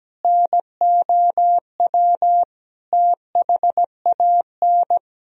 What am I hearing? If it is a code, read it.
Morse code, heard as a keyed tone, about 17 wpm, 700 Hz. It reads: NOW THAN
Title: Sentences from Top 1000 Words - Review Part 4 - 17wpm